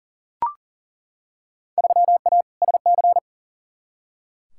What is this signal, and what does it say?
Morse code, heard as a keyed tone, about 40 wpm, 700 Hz. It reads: 3A SC
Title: Field Day Exchanges - 40wpm